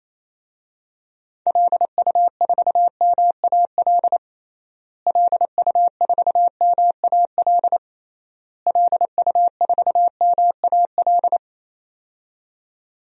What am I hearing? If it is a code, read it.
Morse code, heard as a keyed tone, about 28 wpm, 700 Hz. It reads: LU4MAL LU4MAL LU4MAL
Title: DXCC-100 Call Signs - 6 Characters - Repeated 3x - 28wpm